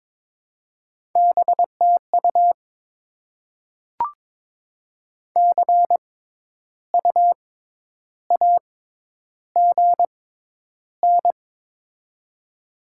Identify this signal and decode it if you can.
Morse code, heard as a keyed tone, about 22 wpm, spaced as for 10 wpm, 700 Hz. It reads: BTU CUAGN
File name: Top 20 QSO Elements - 6x Character Spacing - 22wpm